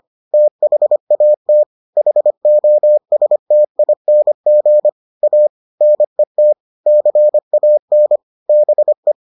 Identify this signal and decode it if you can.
Morse code, heard as a keyed tone, about 25 wpm, 600 Hz. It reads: THAT HOSTING A NET CAN BE